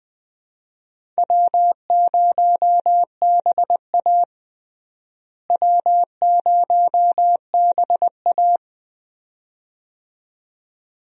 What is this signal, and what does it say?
Morse code, heard as a keyed tone, about 20 wpm, 700 Hz. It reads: W0BA W0BA
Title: USA Call Signs - 4 Characters - Repeated 2x - 20wpm